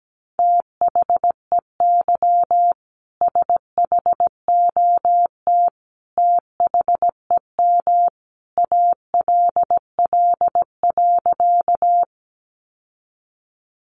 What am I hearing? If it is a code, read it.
Morse code, heard as a keyed tone, about 17 wpm, 700 Hz. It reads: THEY SHOT THEM ALL.
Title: Sentences from Top 1100 Words 17wpm